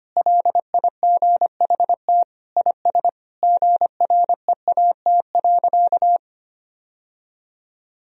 Morse code, heard as a keyed tone, about 25 wpm, 700 Hz. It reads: LIGHT IS GREAT.
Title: Sentences from Top 1000 Words - Review Part 1 - 25wpm